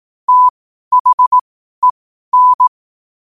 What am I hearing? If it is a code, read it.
Morse code, heard as a keyed tone, about 18 wpm, spaced as for 12 wpm, 1000 Hz. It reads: THEN
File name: qotdfiles / 12wpm